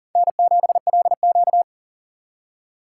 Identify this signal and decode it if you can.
Morse code, heard as a keyed tone, about 40 wpm, 700 Hz. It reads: N7LQ
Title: USA Call Signs - 2x Character Spacing - 40wpm